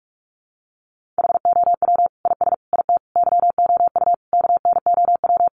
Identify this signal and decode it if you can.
Morse code, heard as a keyed tone, about 44 wpm, 700 Hz. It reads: HOW IS IT YOU KNOW
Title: Sentences from Top 700 Words 45wpm